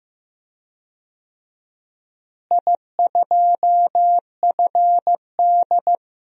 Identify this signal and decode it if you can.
Morse code, heard as a keyed tone, about 15 wpm, 700 Hz. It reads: I2FD
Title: DXCC-100 Call Signs - 4 Characters - Repeated 3x - 15wpm